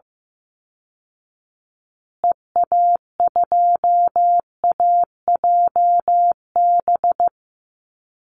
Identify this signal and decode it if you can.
Morse code, heard as a keyed tone, about 14 wpm, 700 Hz. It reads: EA2AJB
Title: Call Signs Repeated 2x - 15wpm